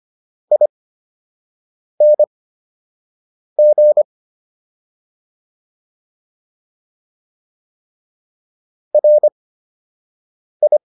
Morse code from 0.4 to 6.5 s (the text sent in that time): ING